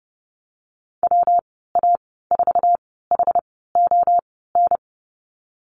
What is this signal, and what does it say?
Morse code, heard as a keyed tone, about 30 wpm, spaced as for 17 wpm, 700 Hz. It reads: WA4HON
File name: Lesson 19g - Letter w - Call Signs 30wpm